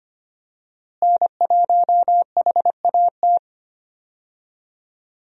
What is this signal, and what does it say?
Morse code, heard as a keyed tone, about 25 wpm, 700 Hz. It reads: N1HAT
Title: USA Call Signs - 5 Characters - Repeated 2x - 25wpm